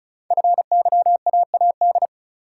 Morse code, heard as a keyed tone, about 36 wpm, 690 Hz. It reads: FYAAD
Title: Code Groups - 5 Letters - 35wpm